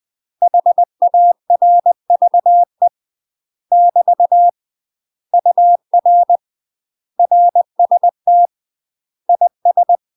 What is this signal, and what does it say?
Morse code, heard as a keyed tone, about 20 wpm, 700 Hz. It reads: HARVE <BT> UR RST IS
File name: UK MASTER QSOs TRAINING / 20WPM / UK MASTER QSOs 0014